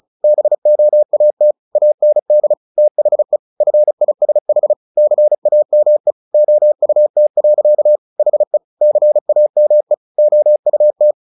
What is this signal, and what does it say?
Morse code, heard as a keyed tone, about 34 wpm, 600 Hz. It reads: BOAT AND THE FISH CAME OUT. HE CAME OUT
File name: the old man and the sea / the old man and the sea 1 35wpm